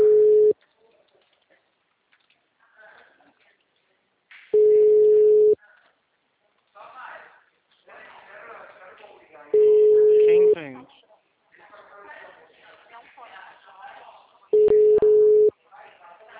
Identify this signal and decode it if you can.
Ringback tone (Europe, 425 Hz), with speech.